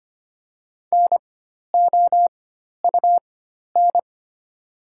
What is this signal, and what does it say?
Morse code, heard as a keyed tone, about 25 wpm, spaced as for 12 wpm, 690 Hz. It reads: NOUN
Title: 4x Character Spacing - Top 500 Words - 25wpm